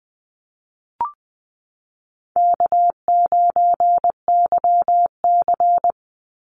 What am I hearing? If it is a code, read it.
Morse code, heard as a keyed tone, about 20 wpm, 700 Hz. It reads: K9YC
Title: Easy Call Signs Repeated 3x - 20wpm